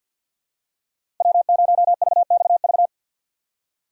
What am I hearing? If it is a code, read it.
Morse code, heard as a keyed tone, about 52 wpm, 700 Hz. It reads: W0VXV
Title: USA Call Signs - 5 Characters - 3x Character Spacing - 50wpm